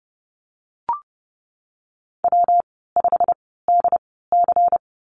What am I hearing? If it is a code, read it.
Morse code, heard as a keyed tone, about 30 wpm, 700 Hz. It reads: W 5 D C
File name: Lesson 19g - Letter w - Call Signs 30wpm